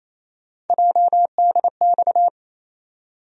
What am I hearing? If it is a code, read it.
Morse code, heard as a keyed tone, about 28 wpm, 700 Hz. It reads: JDX
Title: USA Call Sign Postfixes - 3 Letters 28wpm